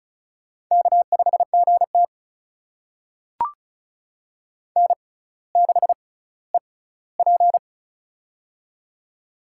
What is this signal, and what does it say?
Morse code, heard as a keyed tone, about 36 wpm, spaced as for 18 wpm, 700 Hz. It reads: K5GT N6EP
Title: USA Call Signs - 4 Characters - 6x Character Spacing - 35wpm